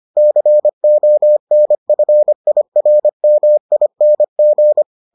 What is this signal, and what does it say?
Morse code, heard as a keyed tone, about 25 wpm, 600 Hz. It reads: CONFIRMING